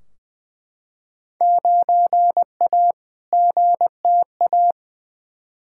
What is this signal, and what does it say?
Morse code, heard as a keyed tone, about 20 wpm, 700 Hz. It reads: 9A GTA
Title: Field Day Exchanges Encore - 20wpm